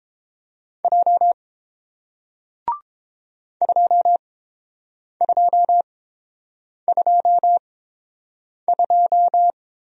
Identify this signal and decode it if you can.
Morse code, heard as a keyed tone, about 31 wpm, spaced as for 7 wpm, 700 Hz. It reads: J 2222